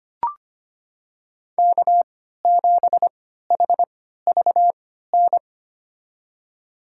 Morse code, heard as a keyed tone, about 25 wpm, spaced as for 14 wpm, 700 Hz. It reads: K7HVN
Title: USA Call Signs - 3x Character Spacing - 25wpm